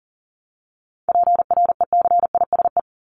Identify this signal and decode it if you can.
Morse code, heard as a keyed tone, about 40 wpm, 700 Hz. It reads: PRECISE
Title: LY Suffix - Top 100 Words with Base Words 40wpm